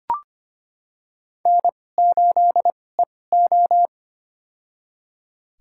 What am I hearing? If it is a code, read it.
Morse code, heard as a keyed tone, about 25 wpm, spaced as for 18 wpm, 700 Hz. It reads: N8EO
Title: USA Call Signs - 2x Character Spacing - 25wpm